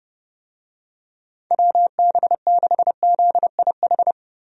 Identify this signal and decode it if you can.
Morse code, heard as a keyed tone, about 32 wpm, 700 Hz. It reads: WB6ZIH
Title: USA Call Signs - 6 Characters - 30wpm